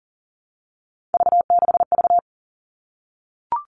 Morse code, heard as a keyed tone, about 40 wpm, 700 Hz. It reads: V6V
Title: All Cut Numbers - 3-Digit 40wpm